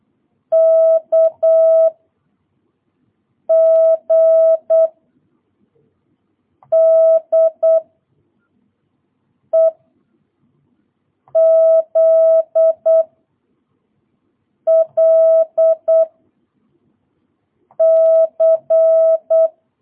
Morse code, heard as a keyed tone, about 8 wpm, 660 Hz. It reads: K G D E Z L C